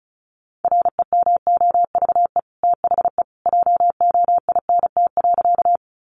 Morse code, heard as a keyed tone, about 34 wpm, 700 Hz. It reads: REMOVE THE JOINT.
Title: Sentences from Top 1300 Words 35wpm